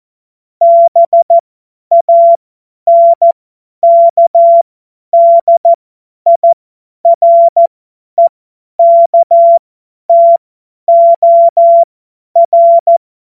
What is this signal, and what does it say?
Morse code, heard as a keyed tone, about 14 wpm, spaced as for 10 wpm, 680 Hz. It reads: BANKDIREKTOR
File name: Morsekurs / Witze / Witz 29 April 2025 50 F70